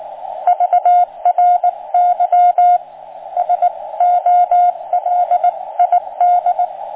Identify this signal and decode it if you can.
Morse code, heard as a keyed tone, about 18 wpm, 710 Hz, with radio noise. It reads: VRY SOLID